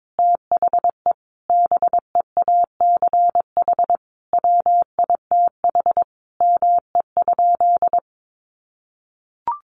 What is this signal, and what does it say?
Morse code, heard as a keyed tone, about 22 wpm, 700 Hz. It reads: THE BEACH WITH ME?
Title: Sentences from Top 1500 Words 22wpm